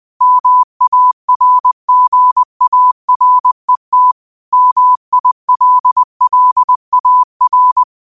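Morse code, heard as a keyed tone, about 20 wpm, 1000 Hz. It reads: MARGARET MILLAR
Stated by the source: Morse quote practice at 20 wpm.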